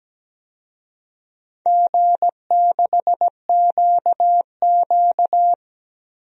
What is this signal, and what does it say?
Morse code, heard as a keyed tone, about 17 wpm, 700 Hz. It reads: G6QQ
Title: DXCC-100 Call Signs - 4 Characters - 17wpm